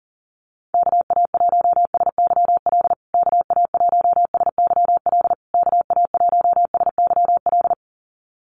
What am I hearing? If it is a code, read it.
Morse code, heard as a keyed tone, about 40 wpm, 700 Hz. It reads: KA1SYL KA1SYL KA1SYL
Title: USA Call Signs Repeated 3x 40wpm